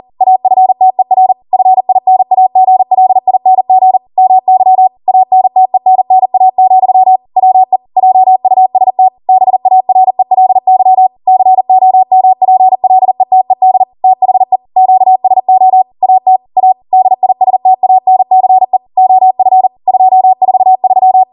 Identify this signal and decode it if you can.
Morse code, heard as a keyed tone, about 40 wpm, 750 Hz. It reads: AFTER FINAGLING MY ANTENNA, WE JUST BARELY COMPLETED THE QSO AT A DISTANCE OF 243